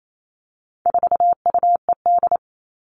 Morse code, heard as a keyed tone, about 28 wpm, 700 Hz. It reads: 4UED
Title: All Cut Numbers - 4-Digit 28wpm